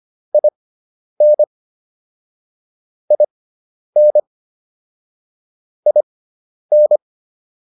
Morse code, heard as a keyed tone, about 25 wpm, spaced as for 10 wpm, 600 Hz. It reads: IN IN IN